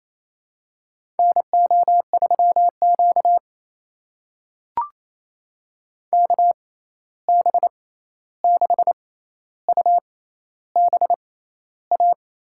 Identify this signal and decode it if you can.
Morse code, heard as a keyed tone, about 27 wpm, spaced as for 13 wpm, 700 Hz. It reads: NO3Q KB6UBA
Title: USA Call Signs - 6x Character Spacing - 28wpm